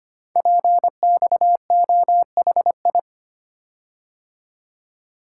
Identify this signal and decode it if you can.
Morse code, heard as a keyed tone, about 25 wpm, 700 Hz. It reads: PXOHI
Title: Code Groups - 5 Letters - 25wpm